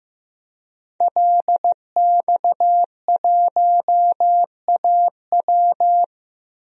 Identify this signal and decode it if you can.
Morse code, heard as a keyed tone, about 15 wpm, 700 Hz. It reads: LX1AW
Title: DXCC-100 Call Signs - 5 Characters - Repeated 3x - 15wpm